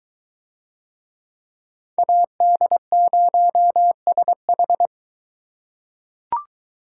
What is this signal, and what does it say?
Morse code, heard as a keyed tone, about 22 wpm, 700 Hz. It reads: AD0SH